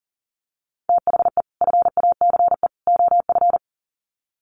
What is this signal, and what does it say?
Morse code, heard as a keyed tone, about 40 wpm, 700 Hz. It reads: THE FACE OF
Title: Sets of 3 Words 40wpm